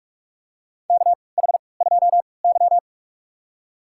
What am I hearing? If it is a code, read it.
Morse code, heard as a keyed tone, about 47 wpm, spaced as for 26 wpm, 700 Hz. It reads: KH2Y
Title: USA Call Signs - 4 Characters - 3x Character Spacing - 45wpm